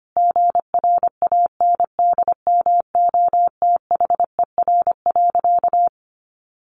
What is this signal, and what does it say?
Morse code, heard as a keyed tone, about 25 wpm, 700 Hz. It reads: GRANDMOTHER.
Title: Sentences from Top 1200 Words 25wpm